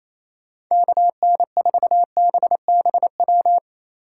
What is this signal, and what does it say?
Morse code, heard as a keyed tone, about 34 wpm, 700 Hz. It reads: KN4BBW
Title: Speed Racing - USA 2x3 Call Signs 28wpm